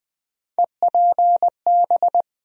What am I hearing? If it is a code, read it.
Morse code, heard as a keyed tone, about 20 wpm, 700 Hz. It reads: EPB
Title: USA Call Sign Postfixes - 3 Letters 20wpm